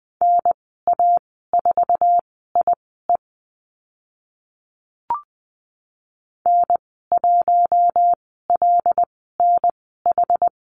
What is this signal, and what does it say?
Morse code, heard as a keyed tone, about 20 wpm, spaced as for 14 wpm, 700 Hz. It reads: NA4IE N1LNH
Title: 076 Lesson 13g - Letter L - Call Signs 20wpm